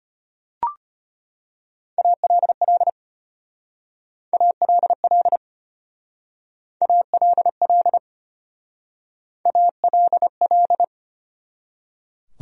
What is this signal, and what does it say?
Morse code, heard as a keyed tone, about 32 wpm, 700 Hz. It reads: ALL ALL ALL ALL